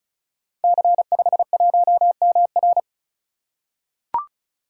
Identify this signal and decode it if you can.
Morse code, heard as a keyed tone, about 36 wpm, 690 Hz. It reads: C51MR
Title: Code Groups - 5 Letter:Numbers - 35wpm